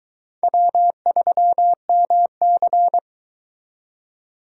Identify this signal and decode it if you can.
Morse code, heard as a keyed tone, about 23 wpm, 700 Hz. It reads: W3MC